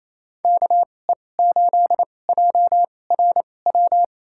Morse code, heard as a keyed tone, about 28 wpm, spaced as for 20 wpm, 700 Hz. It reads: KE8JRW
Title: USA Call Signs - 2x Character Spacing - 28wpm